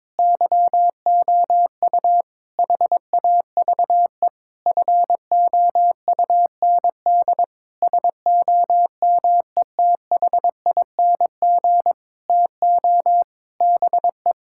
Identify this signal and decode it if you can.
Morse code, heard as a keyed tone, about 22 wpm, 700 Hz. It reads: YOU HAVE FOUND SOMETHING TO BE
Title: Sentences from Top 1000 Words - Review Part 6 - 22wpm